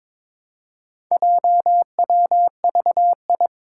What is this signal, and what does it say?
Morse code, heard as a keyed tone, about 22 wpm, 700 Hz. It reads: JWVI
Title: Four Challenging Letter-Number Combinations 22wpm